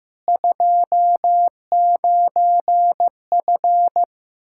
Morse code, heard as a keyed tone, about 15 wpm, 700 Hz. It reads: 29F
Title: Practice QSO - Exchange 2 - Info - 15wpm